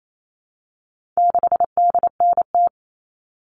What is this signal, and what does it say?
Morse code, heard as a keyed tone, about 28 wpm, 700 Hz. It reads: 6DNT
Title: All Cut Numbers - 4-Digit 28wpm